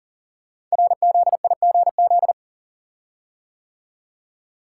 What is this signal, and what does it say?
Morse code, heard as a keyed tone, about 40 wpm, 700 Hz. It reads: RZIGZ
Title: Code Groups - 5 Letters - 40wpm